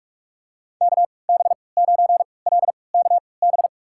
Morse code, heard as a keyed tone, about 46 wpm, spaced as for 26 wpm, 700 Hz. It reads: KB9LKB
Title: USA Call Signs - 6 Characters - 3x Character Spacing - 45wpm